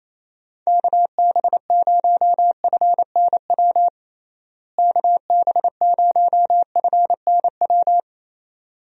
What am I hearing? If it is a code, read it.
Morse code, heard as a keyed tone, about 28 wpm, 700 Hz. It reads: KB0FNW KB0FNW
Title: USA Call Signs - 6 Characters - Repeated 2x - 28wpm